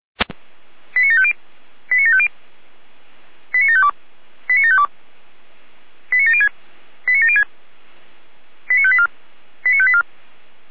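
Five-tone selective call (ZVEI), heard as quick runs of five tones, tones of about 70 ms. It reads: 89750 89750 89752 89752 89706 89706 89584 89584
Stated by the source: Five-tone ZVEI selective call 89531 received on an SDR.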